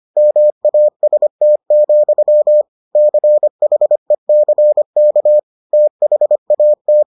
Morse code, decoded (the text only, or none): MAST, CHECK THAT